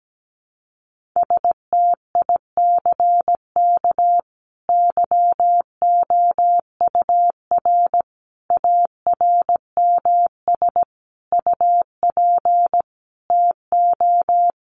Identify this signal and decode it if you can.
Morse code, heard as a keyed tone, about 17 wpm, 700 Hz. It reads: STICK YOUR ARMS UP TO
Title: A Sound Of Thunder 17wpm